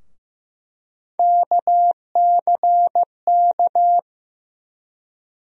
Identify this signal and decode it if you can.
Morse code, heard as a keyed tone, about 16 wpm, 700 Hz. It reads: KCK
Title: USA Call Sign Postfixes - 3 Letters 15wpm